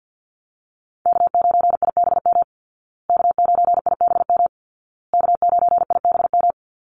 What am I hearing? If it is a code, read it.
Morse code, heard as a keyed tone, about 50 wpm, 700 Hz. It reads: K9IBM K9IBM K9IBM
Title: Call Signs Repeated 2x - 50wpm